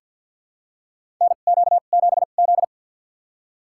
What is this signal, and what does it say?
Morse code, heard as a keyed tone, about 53 wpm, spaced as for 37 wpm, 700 Hz. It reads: NQ7Z
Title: USA Call Signs - 4 Characters - 2x Character Spacing - 50wpm